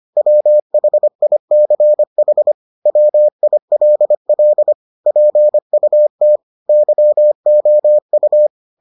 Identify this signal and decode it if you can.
Morse code, heard as a keyed tone, about 25 wpm, 600 Hz. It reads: WHICH WILL PUT YOU